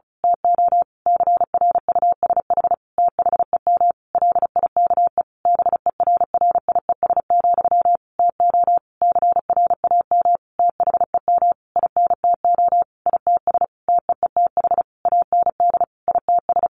Morse code, heard as a keyed tone, about 34 wpm, 700 Hz. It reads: TO CRUSH THEM LIKE BERRIES, TO CRAM THEM INTO ITS TEETH AND ITS
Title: A Sound Of Thunder 35wpm